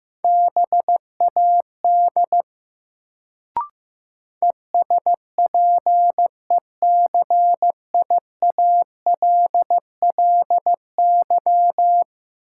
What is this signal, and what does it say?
Morse code, heard as a keyed tone, about 15 wpm, 700 Hz. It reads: BAD ESPECIALLY
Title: Sentences from Top 700 Words - New Word Drill - 15wpm